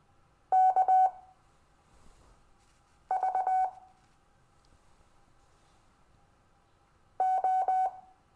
Morse code, heard as a keyed tone, about 20 wpm, spaced as for 4 wpm, 750 Hz. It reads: KV O
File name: cw / Lesson15 five forwardslash